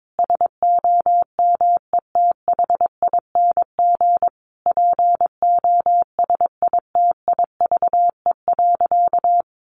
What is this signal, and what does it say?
Morse code, heard as a keyed tone, about 22 wpm, 700 Hz. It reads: SOMETHING POSITIVE.
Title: Sentences from Top 1400 Words 22wpm